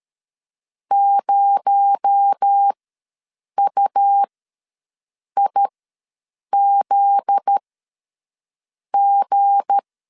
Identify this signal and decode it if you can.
Morse code, heard as a keyed tone, about 13 wpm, spaced as for 6 wpm, 800 Hz. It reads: 0UIZG
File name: LEZIONI / 7 GF9Z / 7 GF9Z ws 12wpm 3321